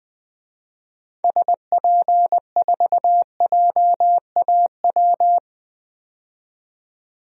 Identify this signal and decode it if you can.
Morse code, heard as a keyed tone, about 21 wpm, 700 Hz. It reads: SP4JAW
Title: DXCC-100 Call Signs - 6 Characters - 20wpm